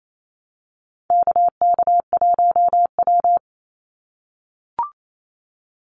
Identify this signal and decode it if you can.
Morse code, heard as a keyed tone, about 28 wpm, 700 Hz. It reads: KK1W
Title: NA SOTA Activator Callsigns 28wpm